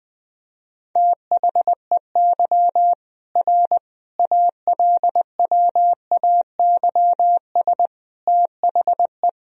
Morse code, heard as a keyed tone, about 21 wpm, 700 Hz. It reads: THEY R ALWAYS THE
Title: Sentences from Top 300 Words and Abbreviations 20wpm